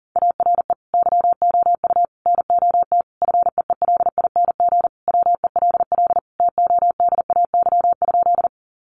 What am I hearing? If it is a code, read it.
Morse code, heard as a keyed tone, about 40 wpm, 700 Hz. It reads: ARE YOU NOT FEELING WELL TODAY?
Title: Sentences from Top 1200 Words 40wpm